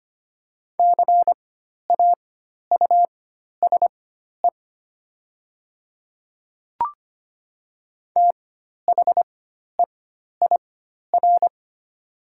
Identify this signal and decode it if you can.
Morse code, heard as a keyed tone, about 25 wpm, spaced as for 12 wpm, 700 Hz. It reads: CAUSE THEIR
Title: 4x Character Spacing - Top 200 Words - 25wpm